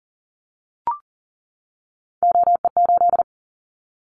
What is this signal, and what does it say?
Morse code, heard as a keyed tone, about 40 wpm, 700 Hz. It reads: OE8
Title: Callsign Prefixes for Top 100 DXCC - 40wpm